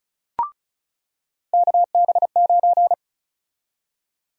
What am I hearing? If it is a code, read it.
Morse code, heard as a keyed tone, about 34 wpm, 700 Hz. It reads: KB9
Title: USA Call Sign Prefixes - 35wpm